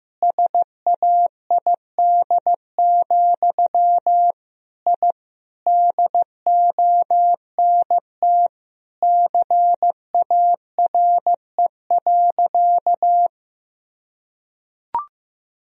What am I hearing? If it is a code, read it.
Morse code, heard as a keyed tone, about 16 wpm, 700 Hz. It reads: SAID, I DONT CARE.
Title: Sentences from Top 1000 Words - Review Part 1 - 15wpm